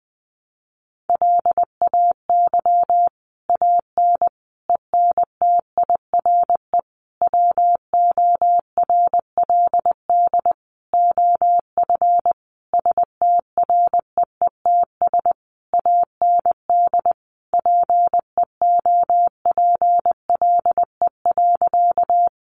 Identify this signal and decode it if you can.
Morse code, heard as a keyed tone, about 20 wpm, 700 Hz. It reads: LAY AN ENTIRE WORLD OF STREETS AND PEOPLE.
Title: A Sound Of Thunder 20wpm